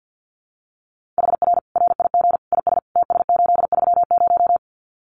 Morse code, heard as a keyed tone, about 50 wpm, 700 Hz. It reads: HR RIG IS TS830
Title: Common QSO Phrases 50wpm